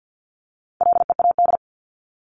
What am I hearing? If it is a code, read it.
Morse code, heard as a keyed tone, about 49 wpm, 700 Hz. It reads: LEAD
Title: ING Suffix - Top 100 Words with Base Words 50wpm